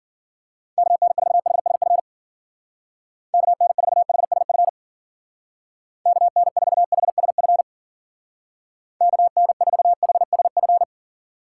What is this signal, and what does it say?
Morse code, heard as a keyed tone, about 50 wpm, 700 Hz. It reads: KN4HSF KN4HSF KN4HSF KN4HSF
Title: Speed Racing - USA 2x3 Call Signs 40wpm